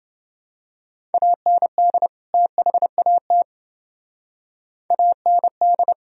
Morse code, heard as a keyed tone, about 29 wpm, 700 Hz. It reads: AND THAT AND